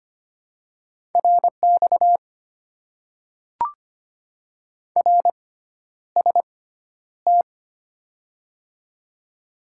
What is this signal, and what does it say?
Morse code, heard as a keyed tone, about 25 wpm, spaced as for 11 wpm, 700 Hz. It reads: RX RST
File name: Top 90 QSO Elements - 6x Character Spacing - 25wpm